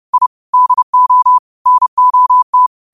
Morse code, heard as a keyed tone, about 30 wpm, 1000 Hz. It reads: I DO NOT
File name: qotdfiles / 30wpm